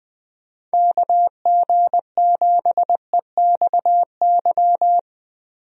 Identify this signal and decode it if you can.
Morse code, heard as a keyed tone, about 20 wpm, 700 Hz. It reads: KG7EXY
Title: USA Call Signs - 6 Characters - Repeated 2x - 20wpm